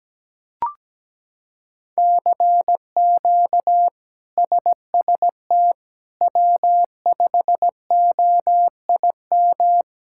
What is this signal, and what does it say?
Morse code, heard as a keyed tone, about 17 wpm, 700 Hz. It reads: CQ SST W5OIM
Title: K1USN SST QSO Practice - 17wpm